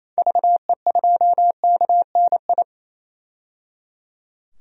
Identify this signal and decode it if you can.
Morse code, heard as a keyed tone, about 28 wpm, 700 Hz. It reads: VE2KNI
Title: DXCC-100 Call Signs - 6 Characters - 28wpm